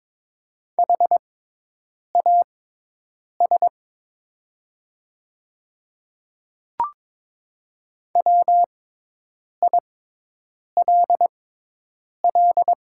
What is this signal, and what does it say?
Morse code, heard as a keyed tone, about 22 wpm, spaced as for 8 wpm, 700 Hz. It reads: HAS WILL
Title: 6x Character Spacing - Top 100 Words - 22wpm